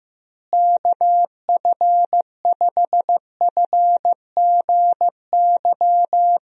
Morse code, heard as a keyed tone, about 15 wpm, 700 Hz. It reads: KF5FGY
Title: USA Call Signs - 6 Characters - Repeated 2x - 15wpm